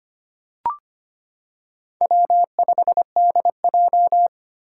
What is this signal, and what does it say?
Morse code, heard as a keyed tone, about 25 wpm, 700 Hz. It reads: W5DJ
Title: Four Challenging Letter-Number Combinations 25wpm